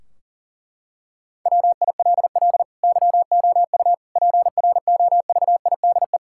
Morse code, heard as a keyed tone, about 40 wpm, 700 Hz. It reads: WILL YOU PROVIDE
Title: Sentences from Top 1600 Words 40wpm